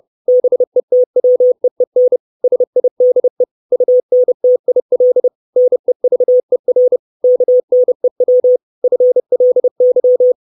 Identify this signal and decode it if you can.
Morse code, heard as a keyed tone, about 30 wpm, 500 Hz. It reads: BETWEEN SIDE UNTIL NEVER KNEW FLY
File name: 632 Words 30WPM